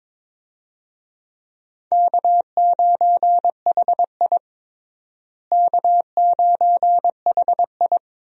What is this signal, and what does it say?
Morse code, heard as a keyed tone, about 22 wpm, 700 Hz. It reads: K9HI K9HI
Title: USA Call Signs - 4 Characters - Repeated 2x - 22wpm